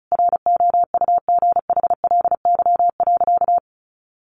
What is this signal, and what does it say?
Morse code, heard as a keyed tone, about 34 wpm, 700 Hz. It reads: ROUGHLY.
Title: Sentences from Top 1500 Words 35wpm